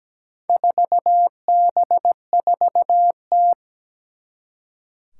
Morse code, heard as a keyed tone, about 17 wpm, 700 Hz. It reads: 4B4T
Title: Four Challenging Letter-Number Combinations 17wpm